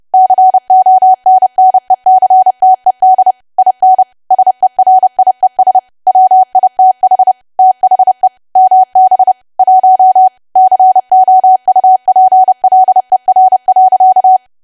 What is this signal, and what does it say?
Morse code, heard as a keyed tone, about 30 wpm, 750 Hz. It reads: CONNECTED IN SERIES WITH THE MB 1 COUPLER.
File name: cwa / W1AW-Level3 / 30WPM